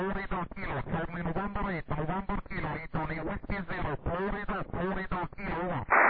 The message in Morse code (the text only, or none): none